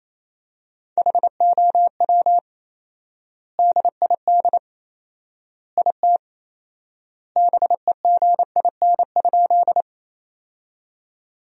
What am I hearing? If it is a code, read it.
Morse code, heard as a keyed tone, about 28 wpm, 700 Hz. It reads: HOW DID IT BEGIN?
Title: 4x Word-Spacing - Sentences from Top 300 Words - 28wpm